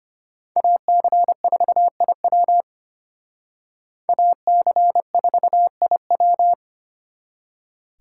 Morse code, heard as a keyed tone, about 32 wpm, 700 Hz. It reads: AC4IW AC4IW